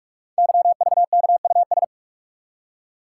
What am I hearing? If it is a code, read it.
Morse code, heard as a keyed tone, about 44 wpm, 690 Hz. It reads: YVKUS
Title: Code Groups - 5 Letters - 45wpm